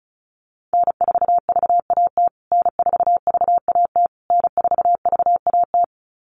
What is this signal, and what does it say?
Morse code, heard as a keyed tone, about 34 wpm, 700 Hz. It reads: N4VAT N4VAT N4VAT
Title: USA Call Signs Repeated 3x 35wpm